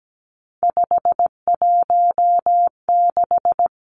Morse code, heard as a keyed tone, about 17 wpm, 700 Hz. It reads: 516
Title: Numbers - 3-Digit 17wpm